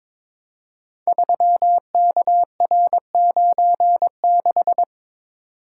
Morse code, heard as a keyed tone, about 22 wpm, 700 Hz. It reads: 3KR96
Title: Code Groups - 5 Letter:Numbers - 22wpm